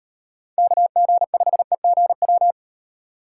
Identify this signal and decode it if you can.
Morse code, heard as a keyed tone, about 31 wpm, 700 Hz. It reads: KG5EGW